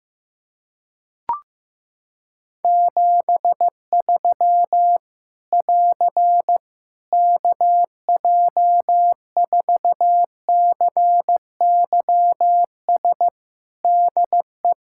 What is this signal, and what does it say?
Morse code, heard as a keyed tone, about 15 wpm, 700 Hz. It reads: 73 <AR> KJ4CYS DE
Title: Practice QSO - Exchange 4 - Ending - 15wpm